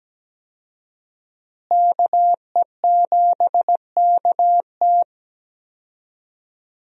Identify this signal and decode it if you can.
Morse code, heard as a keyed tone, about 17 wpm, 700 Hz. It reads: KE7KT
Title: USA Call Signs - 5 Characters - Repeated 3x - 17wpm